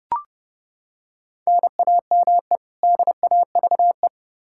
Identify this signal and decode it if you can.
Morse code, heard as a keyed tone, about 32 wpm, 700 Hz. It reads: NAME DAVE
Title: Practice QSO - Exchange 1 - Beginning - 30wpm